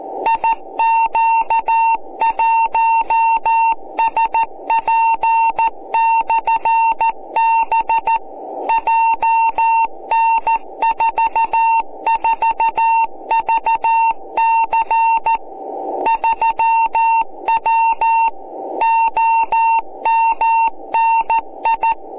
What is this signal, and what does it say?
Morse code, heard as a keyed tone, about 14 wpm, 910 Hz. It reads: IQ1SP/B JN44VC 3W OMNI